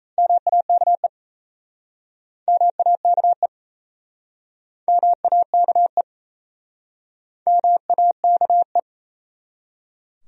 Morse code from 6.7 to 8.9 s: MAKE

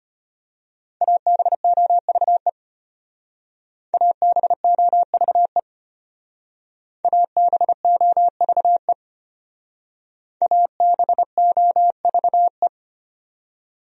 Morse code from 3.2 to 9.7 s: ABOVE ABOVE

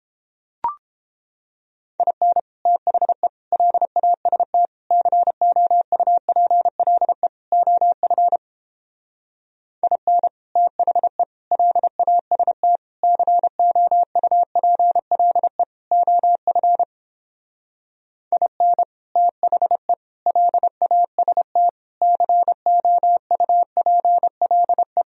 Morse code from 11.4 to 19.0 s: LAST COUPLE OF IN

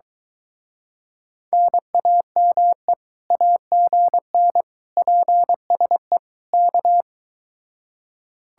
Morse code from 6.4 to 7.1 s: K

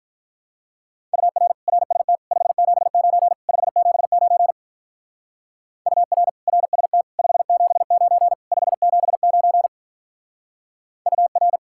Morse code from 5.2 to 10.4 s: UR RST 579 579

